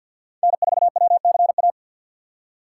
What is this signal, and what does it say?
Morse code, heard as a keyed tone, about 48 wpm, 700 Hz. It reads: N4WCA
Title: USA Call Signs - 5 Characters - 6x Character Spacing - 50wpm